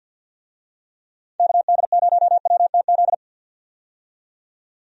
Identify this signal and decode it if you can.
Morse code, heard as a keyed tone, about 48 wpm, 700 Hz. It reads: KD0WTZ
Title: USA Call Signs - 6 Characters - 50wpm